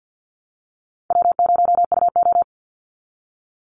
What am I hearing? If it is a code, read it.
Morse code, heard as a keyed tone, about 50 wpm, 700 Hz. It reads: W0UO
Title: Call Signs Repeated 2x - 50wpm